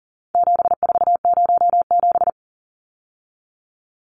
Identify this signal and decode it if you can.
Morse code, heard as a keyed tone, about 40 wpm, 700 Hz. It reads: 7407
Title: Numbers - 4-Digit 40wpm